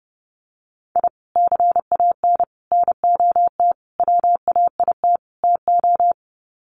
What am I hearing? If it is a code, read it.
Morse code, heard as a keyed tone, about 30 wpm, 700 Hz. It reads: I CAN NOT WAIT TO
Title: Sets of 5 Words Encore 30wpm